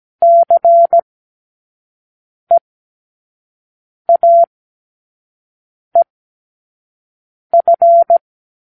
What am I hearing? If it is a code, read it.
Morse code, heard as a keyed tone, about 17 wpm, spaced as for 5 wpm, 680 Hz. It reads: CEAEF
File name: k7qo / Lesson9(A-F)